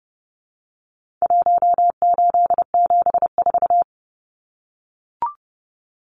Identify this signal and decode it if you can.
Morse code, heard as a keyed tone, about 30 wpm, 700 Hz. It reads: 1874
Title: Numbers - Contest Style - 4-Digit 30wpm